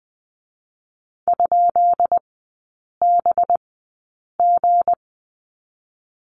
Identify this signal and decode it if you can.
Morse code, heard as a keyed tone, about 20 wpm, 700 Hz. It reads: ? B G
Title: 212 Lesson 34e - Number 8 - Three Characters 20wpm